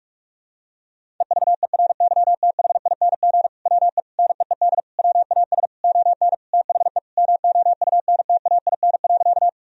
Morse code, heard as a keyed tone, about 42 wpm, 700 Hz. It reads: EVERYTHING WE NEED WAS ON THE MOUNTAIN.
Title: Sentences from Top 1000 Words - Review Part 1 - 45wpm